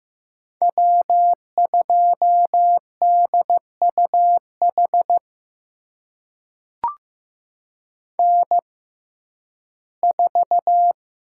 Morse code, heard as a keyed tone, about 15 wpm, spaced as for 11 wpm, 700 Hz. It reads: W2DUH N4
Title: USA Call Signs - 5 Characters - 6x Character Spacing - 15wpm